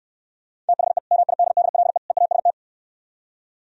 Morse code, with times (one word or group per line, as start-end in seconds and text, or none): 0.7-1.0 THE
1.1-2.0 MIDDLE
2.1-2.5 EAST